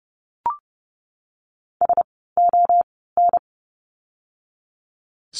Morse code, heard as a keyed tone, about 30 wpm, 700 Hz. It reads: S O N
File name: Lesson 16f - Letter c - Words 30wpm